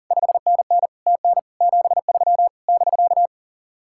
Morse code, heard as a keyed tone, about 49 wpm, 700 Hz. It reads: 5NN TN 73 <BK>
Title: Speed-Racing Common QSO Phrases - 40wpm